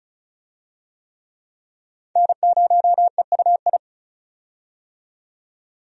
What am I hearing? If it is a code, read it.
Morse code, heard as a keyed tone, about 34 wpm, 700 Hz. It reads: N0EUI
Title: USA Call Signs - 5 Characters - 35wpm